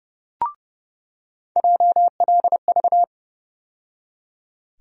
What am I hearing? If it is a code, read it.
Morse code, heard as a keyed tone, about 32 wpm, 700 Hz. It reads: JLV
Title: USA Call Sign Postfixes - 3 Letters 30wpm